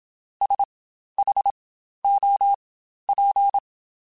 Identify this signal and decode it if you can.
Morse code, heard as a keyed tone, about 27 wpm, spaced as for 12 wpm, 800 Hz. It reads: SHOP
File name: Spaced Characters / Spaced Characters-mp3s / Weather4 24 sp @25